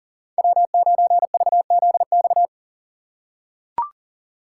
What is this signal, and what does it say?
Morse code, heard as a keyed tone, about 40 wpm, 700 Hz. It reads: W9VZX
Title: USA Call Signs - 6x Character Spacing - 40wpm